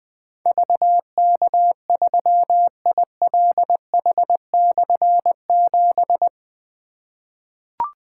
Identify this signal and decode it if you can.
Morse code, heard as a keyed tone, about 20 wpm, 700 Hz. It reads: VK3ILH/7
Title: Difficult Call Signs 20wpm